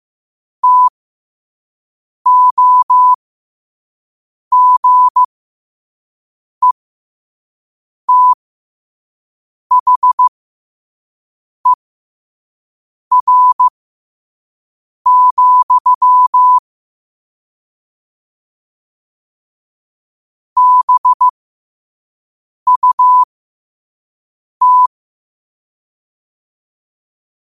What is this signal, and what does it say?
Morse code, heard as a keyed tone, about 15 wpm, spaced as for 5 wpm, 1000 Hz. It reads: TOGETHER, BUT
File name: qotdfiles / 5wpm